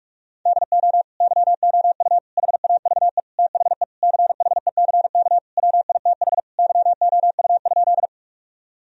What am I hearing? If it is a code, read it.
Morse code, heard as a keyed tone, about 41 wpm, 700 Hz. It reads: DO YOU HAVE THE CHECK WITH YOU?
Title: Sentences from Top 1000 Words - Review Part 3 - 45wpm